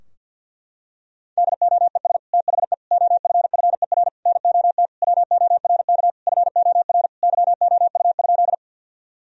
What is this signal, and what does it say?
Morse code, heard as a keyed tone, about 50 wpm, 700 Hz. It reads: DOES THE OFFER NOT WORK FOR YOU?
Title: Sentences from Top 1000 Words - Review Part 4 - 50wpm